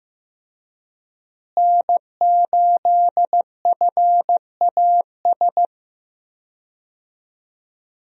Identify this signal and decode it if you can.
Morse code, heard as a keyed tone, about 15 wpm, 700 Hz. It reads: N8FAS
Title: USA Call Signs - 5 Characters - Repeated 2x - 15wpm